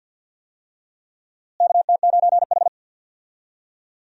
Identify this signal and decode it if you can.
Morse code, heard as a keyed tone, about 48 wpm, 700 Hz. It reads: KT9H
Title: USA Call Signs - 4 Characters - Repeated 3x - 50wpm